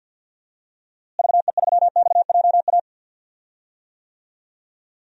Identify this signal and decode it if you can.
Morse code, heard as a keyed tone, about 53 wpm, 700 Hz. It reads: VE3XJA
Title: DXCC-100 Call Signs - 6 Characters - Repeated 2x - 50wpm